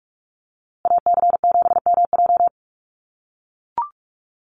Morse code, heard as a keyed tone, about 44 wpm, 700 Hz. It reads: AC7MJ
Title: USA Call Signs Repeated 2x 45wpm